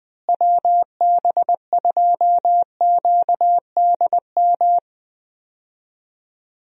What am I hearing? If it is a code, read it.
Morse code, heard as a keyed tone, about 21 wpm, 700 Hz. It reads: WB2QDM
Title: USA Call Signs - 6 Characters - Repeated 3x - 20wpm